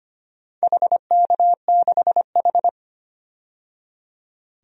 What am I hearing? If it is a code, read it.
Morse code, heard as a keyed tone, about 25 wpm, 700 Hz. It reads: HK6H
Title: DXCC-100 Call Signs - 4 Characters - 25wpm